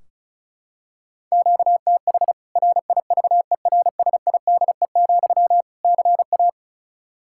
Morse code, heard as a keyed tone, about 34 wpm, 700 Hz. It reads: QTH RIVERSIDE, CA
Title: K1USN SST QSO Practice 35wpm